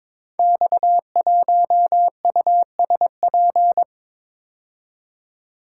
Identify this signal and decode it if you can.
Morse code, heard as a keyed tone, about 22 wpm, 700 Hz. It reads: X1USP
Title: Code Groups - 5 Letter:Numbers - 22wpm